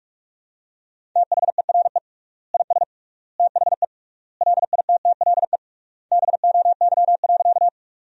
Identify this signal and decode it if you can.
Morse code, heard as a keyed tone, about 45 wpm, 700 Hz. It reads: THERE IS THE LITTLE BOY.
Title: 3x Word-Spacing - Sentences from Top 200 Words - 45wpm